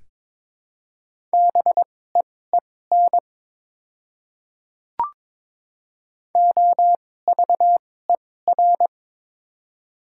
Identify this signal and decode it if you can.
Morse code, heard as a keyed tone, about 22 wpm, spaced as for 16 wpm, 700 Hz. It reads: BEEN OVER
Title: 2x Character Spacing - Top 100 Words - 22wpm